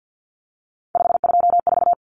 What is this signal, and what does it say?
Morse code, heard as a keyed tone, about 48 wpm, 700 Hz. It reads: 524
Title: Common Cut Numbers - 3-Digit 50wpm